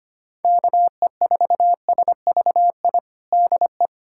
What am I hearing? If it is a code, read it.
Morse code, heard as a keyed tone, about 25 wpm, 700 Hz. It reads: KE4SVI DE
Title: USA Callsign DE Callsign - 6 Characters - 25wpm